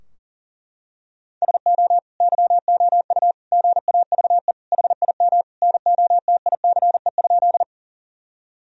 Morse code, heard as a keyed tone, about 40 wpm, 700 Hz. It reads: SO YOU GAVE HIM NOTICE?
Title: Sentences from Top 1000 Words - Review Part 2 - 40wpm